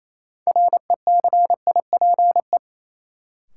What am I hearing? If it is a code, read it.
Morse code, heard as a keyed tone, about 28 wpm, 700 Hz. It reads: RECIPE
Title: Sentences from Top 1200 Words - New Word Drill - 28wpm